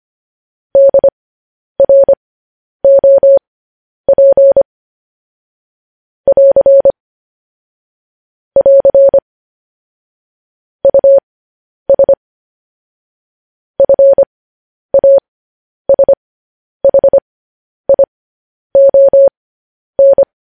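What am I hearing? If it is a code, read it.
Morse code, heard as a keyed tone, about 25 wpm, spaced as for 10 wpm, 550 Hz. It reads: DROP <AR> <AR> US FASHION
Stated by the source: Morse code news headlines broadcast at 10 wpm.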